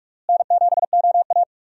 Morse code, heard as a keyed tone, about 46 wpm, 700 Hz. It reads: N7OA
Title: USA Call Signs - 4 Characters - Repeated 3x - 45wpm